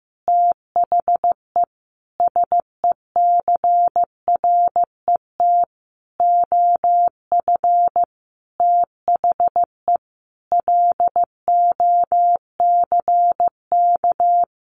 Morse code, heard as a keyed tone, about 15 wpm, 700 Hz. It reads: THE SECRET OF THE LOCK
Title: Sentences from Top 1300 Words 15wpm